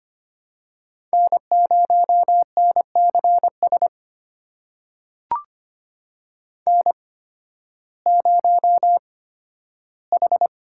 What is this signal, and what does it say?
Morse code, heard as a keyed tone, about 25 wpm, spaced as for 13 wpm, 700 Hz. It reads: N0NCS N0H
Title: USA Call Signs - 5 Characters - 8x Character Spacing - 25wpm